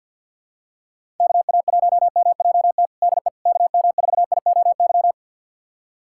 Morse code, heard as a keyed tone, about 52 wpm, 700 Hz. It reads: KA1MJT DE KM4IOY
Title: USA Callsign DE Callsign - 6 Characters - 50wpm